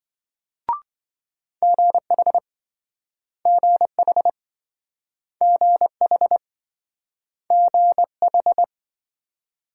Morse code, heard as a keyed tone, about 28 wpm, 700 Hz. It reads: GH GH GH GH